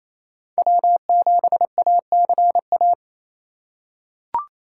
Morse code, heard as a keyed tone, about 28 wpm, 690 Hz. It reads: W7ACA
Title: USA Call Signs - 2x Character Spacing - 28wpm